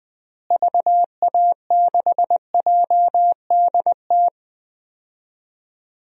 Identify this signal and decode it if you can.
Morse code, heard as a keyed tone, about 20 wpm, 700 Hz. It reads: VA6JDT
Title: DXCC-100 Call Signs - 6 Characters - Repeated 3x - 20wpm